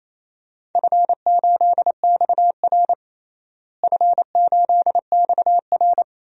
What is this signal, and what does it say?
Morse code, heard as a keyed tone, about 28 wpm, 700 Hz. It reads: F8XR F8XR
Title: DXCC-100 Call Signs - 4 Characters - Repeated 2x - 28wpm